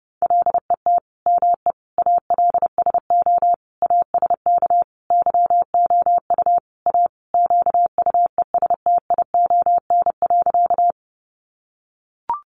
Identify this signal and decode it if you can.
Morse code, heard as a keyed tone, about 30 wpm, 700 Hz. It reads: LET ME ALSO ASK YOU A QUESTION.
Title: Sentences from Top 400 Words 30wpm